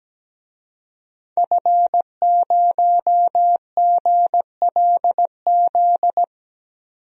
Morse code, heard as a keyed tone, about 17 wpm, 700 Hz. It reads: F0GLZ
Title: DXCC-100 Call Signs - 5 Characters - 17wpm